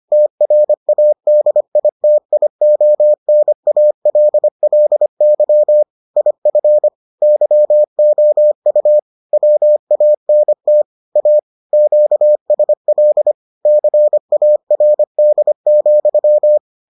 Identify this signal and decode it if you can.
Morse code, heard as a keyed tone, about 25 wpm, 600 Hz. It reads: TRADITIONALLY IF YOU WANT A QSL CARD,